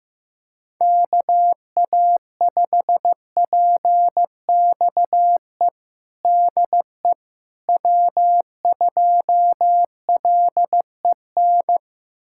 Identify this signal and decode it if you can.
Morse code, heard as a keyed tone, about 15 wpm, 700 Hz. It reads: KA5PXE DE W2LEN
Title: USA Callsign DE Callsign 15wpm